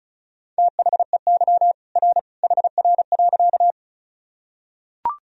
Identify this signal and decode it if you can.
Morse code, heard as a keyed tone, about 36 wpm, 700 Hz. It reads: THEY R HR.
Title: Sentences from Top 200 Words and Abbreviations 35wpm